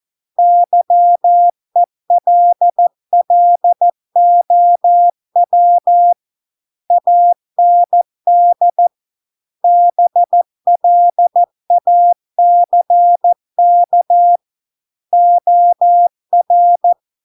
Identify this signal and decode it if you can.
Morse code, heard as a keyed tone, about 14 wpm, 700 Hz. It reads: YELLOW AND BLACK OR